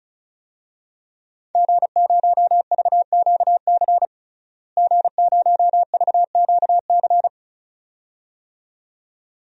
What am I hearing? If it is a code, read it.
Morse code, heard as a keyed tone, about 36 wpm, 700 Hz. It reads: G0VQC G0VQC
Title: DXCC-100 Call Signs - 5 Characters - Repeated 2x - 35wpm